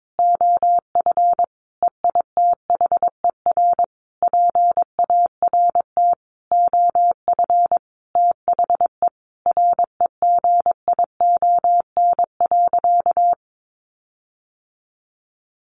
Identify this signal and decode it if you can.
Morse code, heard as a keyed tone, about 22 wpm, 700 Hz. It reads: OF EITHER PART OF THE REGION.
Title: Sentences from Top 1500 Words 22wpm